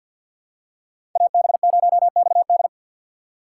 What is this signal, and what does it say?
Morse code, heard as a keyed tone, about 49 wpm, 700 Hz. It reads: AB0XD
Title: USA Call Signs - 3x Character Spacing - 50wpm